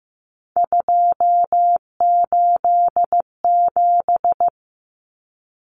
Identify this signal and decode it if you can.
Morse code, heard as a keyed tone, about 15 wpm, 700 Hz. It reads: 287
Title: Numbers - 3-Digit 15wpm